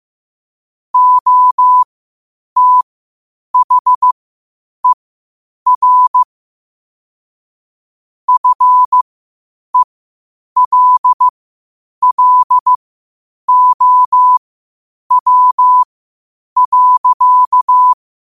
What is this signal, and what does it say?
Morse code, heard as a keyed tone, about 15 wpm, spaced as for 8 wpm, 1000 Hz. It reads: OTHER FELLOW.